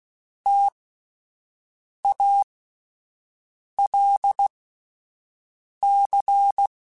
Morse code, heard as a keyed tone, about 16 wpm, spaced as for 5 wpm, 800 Hz. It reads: TALC